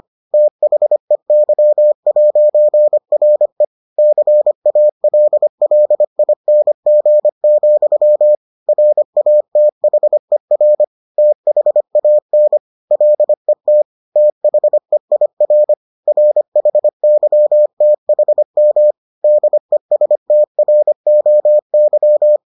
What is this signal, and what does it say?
Morse code, heard as a keyed tone, about 25 wpm, 600 Hz. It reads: THEY'RE CALLING, RATHER THAN LET THEIR RHYTHM DESTROY